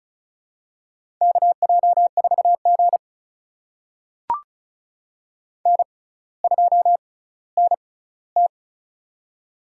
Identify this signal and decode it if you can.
Morse code, heard as a keyed tone, about 36 wpm, spaced as for 18 wpm, 700 Hz. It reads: KJ4G N2NT
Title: USA Call Signs - 4 Characters - 6x Character Spacing - 35wpm